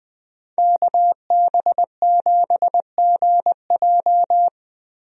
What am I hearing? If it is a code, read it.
Morse code, heard as a keyed tone, about 20 wpm, 700 Hz. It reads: KB7GJ
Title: USA Call Signs - 5 Characters - Repeated 3x - 20wpm